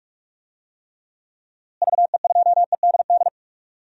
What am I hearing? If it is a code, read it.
Morse code, heard as a keyed tone, about 46 wpm, 700 Hz. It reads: VE2EDD
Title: DXCC-100 Call Signs - 6 Characters - Repeated 2x - 45wpm